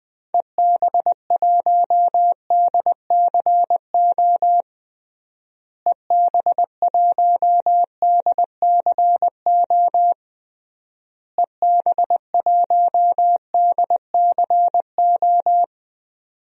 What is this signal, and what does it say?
Morse code, heard as a keyed tone, about 20 wpm, 700 Hz. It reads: EB1DCO EB1DCO EB1DCO
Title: DXCC-100 Call Signs - 6 Characters - Repeated 3x - 20wpm